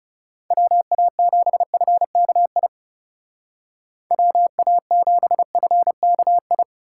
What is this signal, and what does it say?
Morse code, heard as a keyed tone, about 37 wpm, 700 Hz. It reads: WA7FKI WA7FKI
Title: Speed Racing - USA 2x3 Call Signs 30wpm